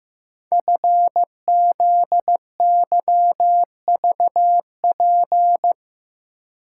Morse code, heard as a keyed tone, about 15 wpm, 700 Hz. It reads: FZYVP
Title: Code Groups - 5 Letter:Numbers - 15wpm